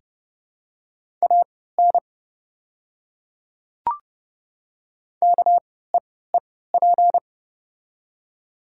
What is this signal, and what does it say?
Morse code, heard as a keyed tone, about 30 wpm, spaced as for 17 wpm, 700 Hz. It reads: AN KEEP
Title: 3x Character Spacing - Top 300 Words - 30wpm